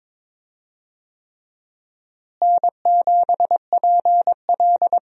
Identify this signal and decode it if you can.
Morse code, heard as a keyed tone, about 22 wpm, 700 Hz. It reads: N7PL
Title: USA Call Signs - 4 Characters - Repeated 3x - 22wpm